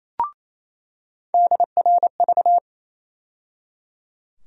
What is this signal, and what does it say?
Morse code, heard as a keyed tone, about 28 wpm, 700 Hz. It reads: DRV
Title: USA Call Sign Postfixes - 3 Letters 28wpm